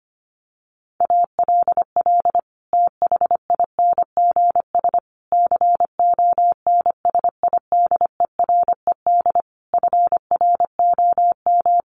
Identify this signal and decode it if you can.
Morse code, heard as a keyed tone, about 25 wpm, 700 Hz. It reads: ALL THINGS CONSIDERED FROM
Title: Sets of 5 Words Encore 25wpm